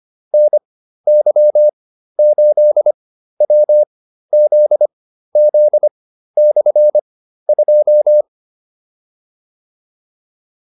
Morse code, heard as a keyed tone, about 25 wpm, spaced as for 13 wpm, 600 Hz. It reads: NY8WZZ/2
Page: intermediate-practice-files